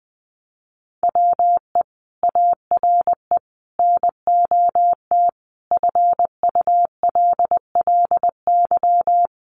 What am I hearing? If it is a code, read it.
Morse code, heard as a keyed tone, about 20 wpm, 700 Hz. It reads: WE ARE NOT FULLY
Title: Sentences from Top 1300 Words 20wpm